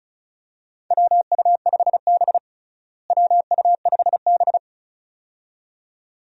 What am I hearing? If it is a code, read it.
Morse code, heard as a keyed tone, about 36 wpm, 700 Hz. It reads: WU5B WU5B
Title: USA Call Signs - 4 Characters - Repeated 2x - 35wpm